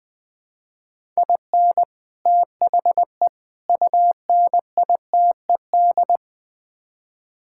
Morse code, heard as a keyed tone, about 21 wpm, 700 Hz. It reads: IN THE UNITED
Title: Build A Phrase - 5-Word Phrases - 20wpm